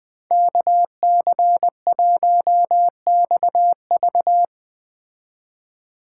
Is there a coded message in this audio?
Yes: Morse code from 0.3 to 4.5 s, heard as a keyed tone, about 20 wpm, 700 Hz. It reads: KC1XV